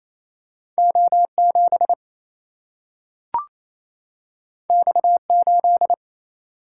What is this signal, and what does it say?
Morse code, heard as a keyed tone, about 28 wpm, 700 Hz. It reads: O 7 X 8
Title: Single-Letter Single-Number 28wpm